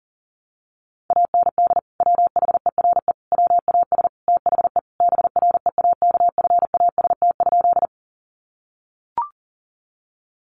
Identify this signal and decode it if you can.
Morse code, heard as a keyed tone, about 40 wpm, 700 Hz. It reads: AND WHERE WAS THE BREAKFAST?
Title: Sentences from Top 1500 Words 40wpm